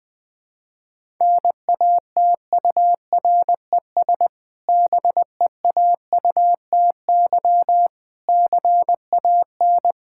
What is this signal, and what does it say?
Morse code, heard as a keyed tone, about 20 wpm, 700 Hz. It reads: NATURES BEAUTY CAN
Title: Difficult Sentences with Hints - 20wpm